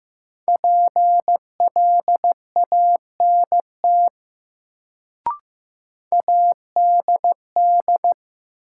Morse code, heard as a keyed tone, about 15 wpm, 700 Hz. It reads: PLANT ADD
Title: Sentences from Top 1000 Words - Review Part 1 - New Word Drill - 15wpm